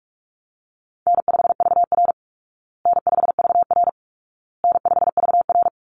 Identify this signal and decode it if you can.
Morse code, heard as a keyed tone, about 44 wpm, 700 Hz. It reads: N5VR N5VR N5VR
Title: Easy Call Signs Repeated 3x - 45wpm